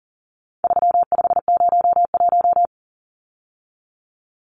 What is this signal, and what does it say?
Morse code, heard as a keyed tone, about 40 wpm, 700 Hz. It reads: 3501
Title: Numbers - 4-Digit 40wpm